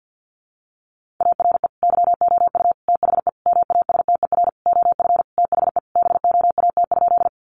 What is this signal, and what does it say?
Morse code, heard as a keyed tone, about 50 wpm, 700 Hz. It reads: ARE YOU THE MASTER OF THE BOAT?
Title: Sentences from Top 1300 Words 50wpm